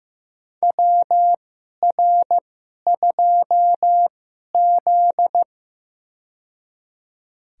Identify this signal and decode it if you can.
Morse code, heard as a keyed tone, about 15 wpm, spaced as for 11 wpm, 700 Hz. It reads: WR2Z
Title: USA Call Signs - 4 Characters - 2x Character Spacing - 15wpm